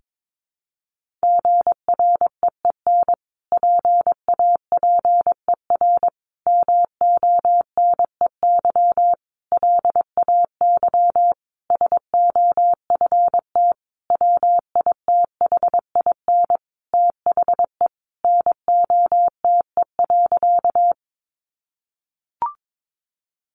Morse code, heard as a keyed tone, about 22 wpm, 700 Hz. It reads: GREEN PAPER MONEY LAY SOFT WITHIN THE NOTE.
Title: Sentences from Top 1500 Words 22wpm